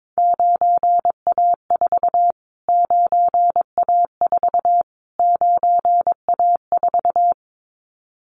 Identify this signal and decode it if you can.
Morse code, heard as a keyed tone, about 22 wpm, 700 Hz. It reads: 9A4 9A4 9A4
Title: Callsign Prefixes for Top 25 DXCC - Repeated 3x - 22wpm